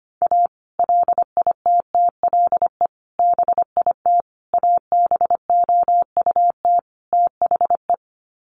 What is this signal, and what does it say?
Morse code, heard as a keyed tone, about 25 wpm, 700 Hz. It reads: A LITTLE BIT ABOUT THE
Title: Sets of 5 Words Encore2 25wpm